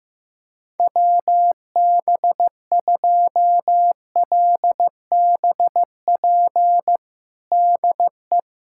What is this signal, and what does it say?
Morse code, heard as a keyed tone, about 16 wpm, 700 Hz. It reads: WB2LBP DE
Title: USA Callsign DE Callsign - 6 Characters - 15wpm